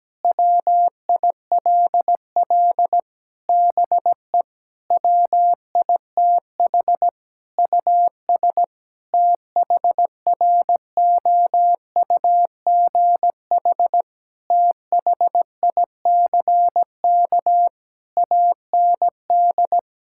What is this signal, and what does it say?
Morse code, heard as a keyed tone, about 17 wpm, 700 Hz. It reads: WILL BE WITH US THROUGH THICK AND
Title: Sentences from Top 1600 Words 17wpm